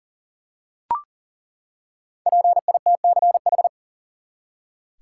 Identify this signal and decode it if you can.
Morse code, heard as a keyed tone, about 40 wpm, 700 Hz. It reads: PITCH
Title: Five Letter Words - 40wpm